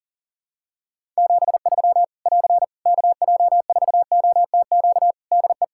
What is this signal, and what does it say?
Morse code, heard as a keyed tone, about 40 wpm, 700 Hz. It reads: 73 <AR> KJ4OTQ DE
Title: Practice QSO - Exchange 4 - Ending - 40wpm